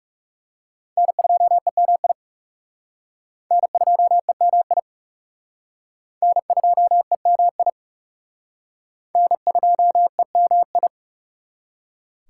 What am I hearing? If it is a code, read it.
Morse code, heard as a keyed tone, about 38 wpm, 700 Hz. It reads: N2EMI N2EMI N2EMI N2EMI